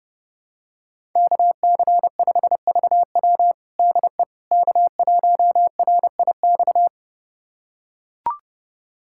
Morse code, heard as a keyed tone, about 32 wpm, 700 Hz. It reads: KC5VW DE K1RIX
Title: USA Callsign DE Callsign 30wpm